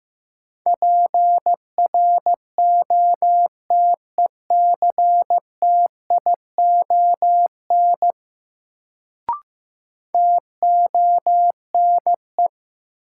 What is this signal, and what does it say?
Morse code, heard as a keyed tone, about 15 wpm, 700 Hz. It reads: PROTECTION TONE
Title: Sentences from Top 1600 Words - New Word Drill Alt - 15wpm